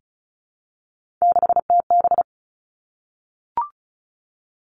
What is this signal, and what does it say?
Morse code, heard as a keyed tone, about 36 wpm, 700 Hz. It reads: 6TB
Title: Three Challenging Letter-Number Combinations 35wpm